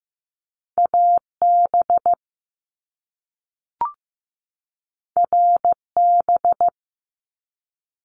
Morse code, heard as a keyed tone, about 15 wpm, 700 Hz. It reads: AB RB
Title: Top Two Letter Combinations in Words 15wpm